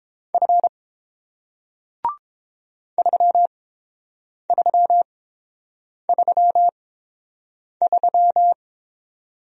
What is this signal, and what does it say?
Morse code, heard as a keyed tone, about 30 wpm, spaced as for 7 wpm, 700 Hz. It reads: F 3333